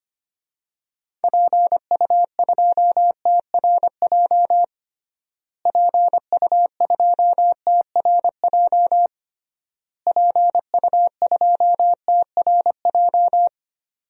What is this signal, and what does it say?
Morse code, heard as a keyed tone, about 25 wpm, 700 Hz. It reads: PU2TRJ PU2TRJ PU2TRJ
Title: DXCC-100 Call Signs - 6 Characters - Repeated 3x - 25wpm